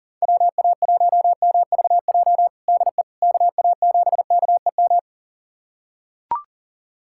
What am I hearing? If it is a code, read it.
Morse code, heard as a keyed tone, about 40 wpm, 700 Hz. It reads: WA1MVJ DE KA7KEM
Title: USA Callsign DE Callsign - 6 Characters - 40wpm